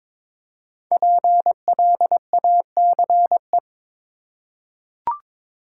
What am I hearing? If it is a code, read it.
Morse code, heard as a keyed tone, about 22 wpm, 700 Hz. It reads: PLACE
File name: 5 Letter Words - First 3 Letters Given - 22wpm